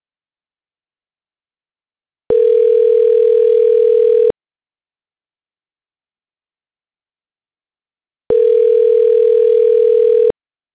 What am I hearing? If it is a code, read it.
Ringback tone (North America), 440 + 480 Hz.